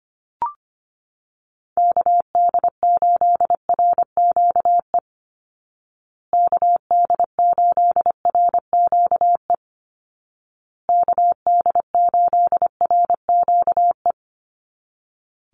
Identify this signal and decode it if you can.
Morse code, heard as a keyed tone, about 25 wpm, 700 Hz. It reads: KD8RQE KD8RQE KD8RQE
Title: Call Signs Repeated 2x - 25wpm